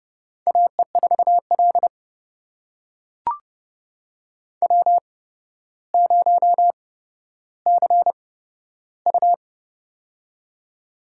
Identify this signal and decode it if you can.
Morse code, heard as a keyed tone, about 31 wpm, spaced as for 13 wpm, 700 Hz. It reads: AE4L W0CU
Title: USA Call Signs - 4 Characters - 8x Character Spacing - 30wpm